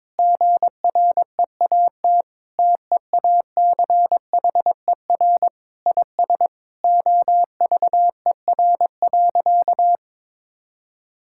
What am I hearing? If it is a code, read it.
Morse code, heard as a keyed tone, about 22 wpm, 700 Hz. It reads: GREAT TEACHER IS OVER.
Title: Sentences from Top 1600 Words 22wpm